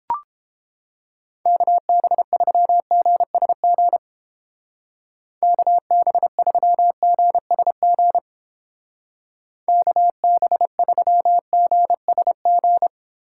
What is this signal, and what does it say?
Morse code, heard as a keyed tone, about 31 wpm, 700 Hz. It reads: KB3GSG KB3GSG KB3GSG